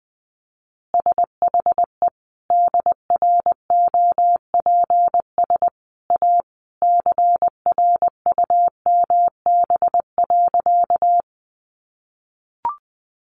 Morse code, heard as a keyed tone, about 20 wpm, 700 Hz. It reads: SHE DROPS A CRUMB.
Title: Easy Phrases 20wpm